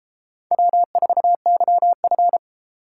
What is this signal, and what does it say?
Morse code, heard as a keyed tone, about 32 wpm, 700 Hz. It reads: W4YF